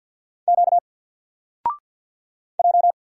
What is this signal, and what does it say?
Morse code, heard as a keyed tone, about 50 wpm, 700 Hz. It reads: Q J
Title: Single Letter-Number 50wpm